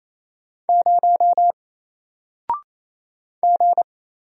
Morse code, heard as a keyed tone, about 28 wpm, 700 Hz. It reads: 0 G
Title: Single Letter-Number Alt 28wpm